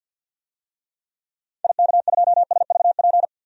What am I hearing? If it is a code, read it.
Morse code, heard as a keyed tone, about 49 wpm, 700 Hz. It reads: IK2SVP
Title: DXCC-100 Call Signs - 6 Characters - 50wpm